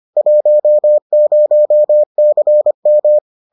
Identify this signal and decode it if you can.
Morse code, heard as a keyed tone, about 25 wpm, 600 Hz. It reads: 10CM